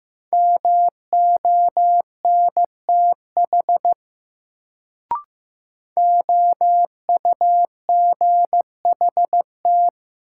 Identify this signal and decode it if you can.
Morse code, heard as a keyed tone, about 15 wpm, 700 Hz. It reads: MONTH OUGHT
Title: Sets of 5 Words Encore - New Word Drill - 15wpm